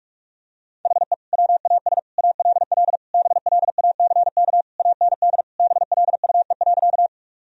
Morse code, heard as a keyed tone, about 46 wpm, 700 Hz. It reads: HE WAS ALL BLACK AND BLUE.
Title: Sentences from Top 1000 Words - Review Part 3 - 45wpm